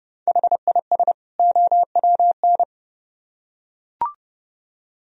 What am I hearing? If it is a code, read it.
Morse code, heard as a keyed tone, about 29 wpm, 700 Hz. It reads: HIS OWN